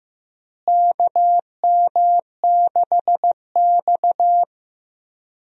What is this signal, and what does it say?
Morse code, heard as a keyed tone, about 15 wpm, 700 Hz. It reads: KM6X
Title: USA Call Signs - 4 Characters - Repeated 3x - 15wpm